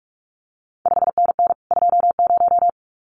Morse code, heard as a keyed tone, about 46 wpm, 700 Hz. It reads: 5NN 20
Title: CQ WW DX Contest Exchange - 45wpm